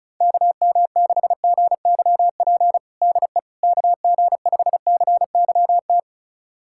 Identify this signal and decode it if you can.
Morse code, heard as a keyed tone, about 34 wpm, 700 Hz. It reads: KM6GYP DE KG5CYT
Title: USA Callsign DE Callsign - 6 Characters - 35wpm